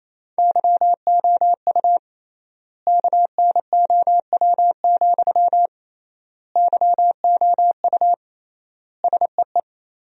Morse code, heard as a keyed tone, about 27 wpm, 700 Hz. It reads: YOU KNOW, YOU SEE
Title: 3x Word-Spacing - Sentences from Top 200 Words - 28wpm